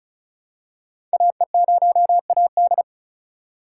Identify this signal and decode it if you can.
Morse code, heard as a keyed tone, about 36 wpm, 690 Hz. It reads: AE0AD
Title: Code Groups - 5 Letter:Numbers - 35wpm